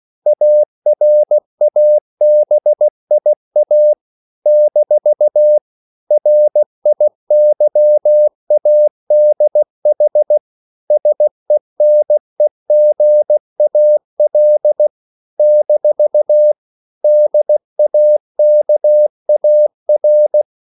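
Morse code, heard as a keyed tone, about 16 wpm, 600 Hz. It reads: ARABIA - RIYADH SENEGAL - DAKAR
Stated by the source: CW practice of national capitals at 16 words per minute, from ON6ZQ.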